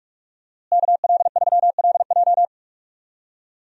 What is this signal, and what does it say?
Morse code, heard as a keyed tone, about 46 wpm, 700 Hz. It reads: KL3LJ
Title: USA Call Signs - 5 Characters - Repeated 2x - 45wpm